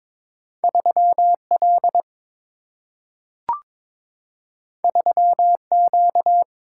Morse code, heard as a keyed tone, about 22 wpm, 700 Hz. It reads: 3L 3Q
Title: Single-Letter Single-Number 22wpm